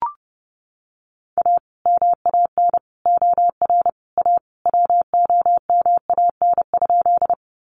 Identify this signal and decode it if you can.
Morse code, heard as a keyed tone, about 30 wpm, 700 Hz. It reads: A MAN OR A WOMAN?
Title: Sentences from Top 1100 Words 30wpm